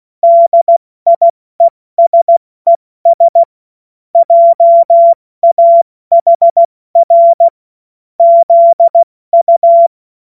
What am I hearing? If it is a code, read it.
Morse code, heard as a keyed tone, about 16 wpm, 680 Hz. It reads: DIESES JAHR ZU
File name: Morsekurs / Witze / Witz 16. Mai 2023 70 F80